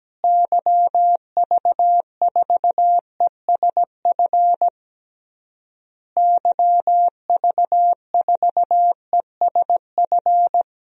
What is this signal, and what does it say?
Morse code, heard as a keyed tone, about 17 wpm, 700 Hz. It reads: YV4ESF YV4ESF
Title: DXCC-100 Call Signs - 6 Characters - Repeated 3x - 17wpm